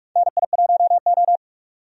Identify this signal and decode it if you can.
Morse code, heard as a keyed tone, about 42 wpm, 700 Hz. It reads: NI1O
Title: USA Call Signs - 4 Characters - 45wpm